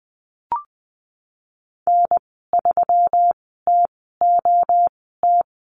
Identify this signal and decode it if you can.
Morse code, heard as a keyed tone, about 20 wpm, spaced as for 14 wpm, 700 Hz. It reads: N3TOT
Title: 147 Lesson 24g - Letter y - Call Signs 20wpm